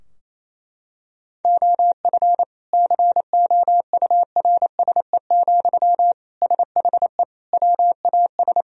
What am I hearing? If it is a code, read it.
Morse code, heard as a keyed tone, about 28 wpm, 700 Hz. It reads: OF COURSE, SHE WAS
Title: Sentences from Top 1000 Words - Review Part 4 - 28wpm